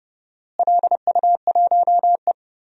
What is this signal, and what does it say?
Morse code, heard as a keyed tone, about 30 wpm, 700 Hz. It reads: LU1E
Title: DXCC-100 Call Signs - 4 Characters - Repeated 3x - 30wpm